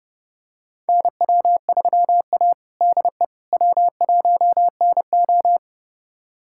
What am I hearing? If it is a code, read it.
Morse code, heard as a keyed tone, about 30 wpm, 700 Hz. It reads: NW3A DE W1NO
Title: USA Callsign DE Callsign - 4 Characters - 30wpm